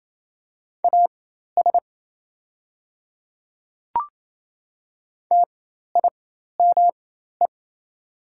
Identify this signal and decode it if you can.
Morse code, heard as a keyed tone, about 27 wpm, spaced as for 13 wpm, 690 Hz. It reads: AS TIME
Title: 4x Character Spacing - Top 200 Words - 28wpm